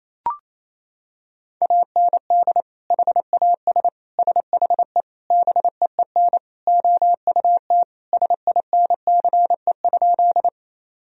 Morse code, heard as a keyed tone, about 28 wpm, 700 Hz. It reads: AND HAS SHE BEEN OUT SINCE?
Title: Sentences from Top 1000 Words - Review Part 1 - 28wpm